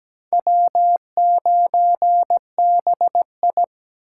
Morse code, heard as a keyed tone, about 17 wpm, 700 Hz. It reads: W9BI
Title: USA Call Signs - 4x Character Spacing - 17wpm